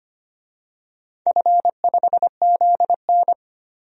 Morse code, heard as a keyed tone, about 25 wpm, 700 Hz. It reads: F5ZN
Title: DXCC-100 Call Signs - 4 Characters - 25wpm